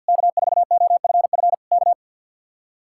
Morse code, heard as a keyed tone, about 64 wpm, 700 Hz. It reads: K4OFF K